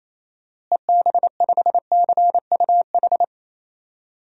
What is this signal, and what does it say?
Morse code, heard as a keyed tone, about 28 wpm, 700 Hz. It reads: EB5CUH
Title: DXCC-100 Call Signs - 6 Characters - Repeated 3x - 28wpm